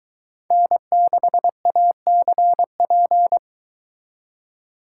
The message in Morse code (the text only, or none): N6ACP